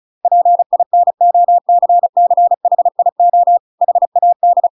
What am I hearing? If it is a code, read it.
Morse code, heard as a keyed tone, about 34 wpm, 700 Hz. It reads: PINOCCHIO HAD